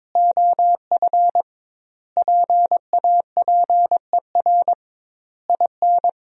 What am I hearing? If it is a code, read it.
Morse code, heard as a keyed tone, about 22 wpm, 700 Hz. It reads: OF PAPER IN
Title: 2x Word-Spacing - Sentences from Top 400 Words - 22wpm